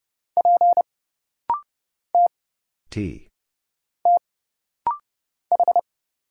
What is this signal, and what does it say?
Morse code, heard as a keyed tone, about 30 wpm, 700 Hz. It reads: P T T H
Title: Single Letter-Number - Rapid-Fire 30wpm